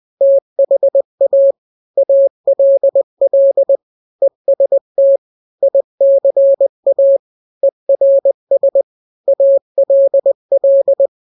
Morse code, heard as a keyed tone, about 20 wpm, 550 Hz. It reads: THA ALL EST ICA ERS ALL
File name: n-grams / 3-grams 20 18 WPM